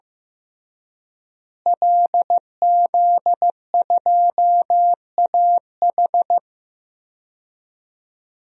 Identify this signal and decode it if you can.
Morse code, heard as a keyed tone, about 15 wpm, 700 Hz. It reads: LZ2AH
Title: DXCC-100 Call Signs - 5 Characters - Repeated 2x - 15wpm